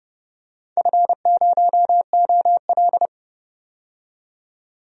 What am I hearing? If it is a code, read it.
Morse code, heard as a keyed tone, about 30 wpm, 700 Hz. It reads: F0OL
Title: DXCC-100 Call Signs - 4 Characters - 30wpm